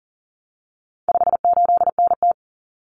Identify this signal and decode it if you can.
Morse code, heard as a keyed tone, about 39 wpm, 700 Hz. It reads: 58NT
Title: Common Cut Numbers - 4-Digit 40wpm